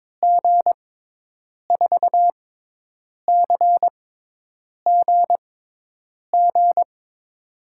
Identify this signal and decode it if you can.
Morse code, heard as a keyed tone, about 22 wpm, spaced as for 8 wpm, 700 Hz. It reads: G4CGG
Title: USA Call Signs - 6 Characters - 6x Character Spacing - 22wpm